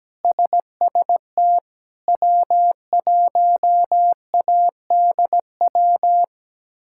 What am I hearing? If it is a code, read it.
Morse code, heard as a keyed tone, about 17 wpm, 700 Hz. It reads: SST W1ADW
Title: K1USN SST QSO Practice - 17wpm